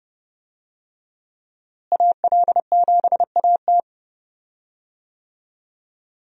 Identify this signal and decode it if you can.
Morse code, heard as a keyed tone, about 32 wpm, 700 Hz. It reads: AL7AT
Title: USA Call Signs - 5 Characters - Repeated 3x - 30wpm